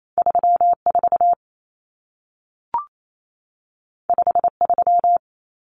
Morse code, heard as a keyed tone, about 28 wpm, 700 Hz. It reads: 34 53
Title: Numbers - 2-Digit 28wpm